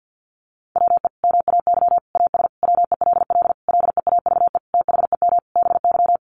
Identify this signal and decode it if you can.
Morse code, heard as a keyed tone, about 50 wpm, 700 Hz. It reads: WE MAY AS WELL LEAVE THEM BY
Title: Sentences from Top 800 Words 50wpm